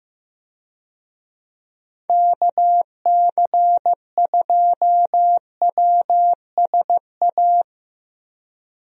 Morse code, heard as a keyed tone, about 15 wpm, 700 Hz. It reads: KC2WSA
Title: USA Call Signs - 6 Characters - 15wpm